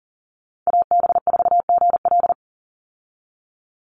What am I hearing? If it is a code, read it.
Morse code, heard as a keyed tone, about 40 wpm, 700 Hz. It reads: AB4GL
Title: USA Call Signs 40wpm